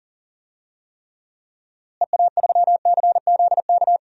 Morse code, heard as a keyed tone, about 40 wpm, 700 Hz. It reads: EA3CZK
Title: DXCC-100 Call Signs - 6 Characters - Repeated 2x - 40wpm